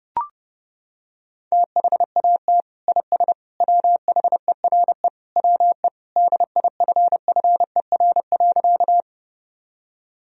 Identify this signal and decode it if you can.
Morse code, heard as a keyed tone, about 30 wpm, 700 Hz. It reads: THAT IS WHERE WE DIFFER.
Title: Sentences from Top 1000 Words - Review Part 1 - 30wpm